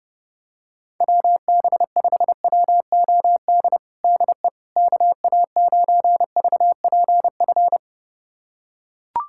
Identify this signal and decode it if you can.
Morse code, heard as a keyed tone, about 30 wpm, 700 Hz. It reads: WB5WOD DE KA9VPF
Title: USA Callsign DE Callsign - 6 Characters - 30wpm